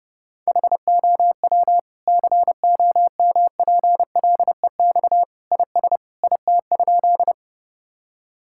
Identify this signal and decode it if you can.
Morse code, heard as a keyed tone, about 29 wpm, 700 Hz. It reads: HOW COMPLEX IS IT?
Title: Sentences from Top 1000 Words - Review Part 6 - 30wpm